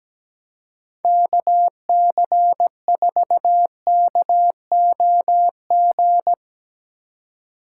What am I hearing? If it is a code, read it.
Morse code, heard as a keyed tone, about 17 wpm, 700 Hz. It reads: KC4KOG
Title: Speed Racing - USA 2x3 Call Signs 17wpm